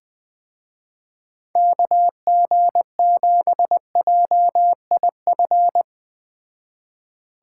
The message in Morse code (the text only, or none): KG7JIF